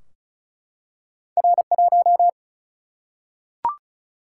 Morse code, heard as a keyed tone, about 36 wpm, 700 Hz. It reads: R 1
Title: Single-Letter Single-Number 35wpm